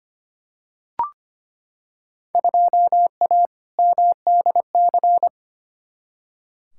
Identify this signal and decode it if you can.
Morse code, heard as a keyed tone, about 25 wpm, 700 Hz. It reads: 2A MDC
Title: Field Day Exchanges - 25wpm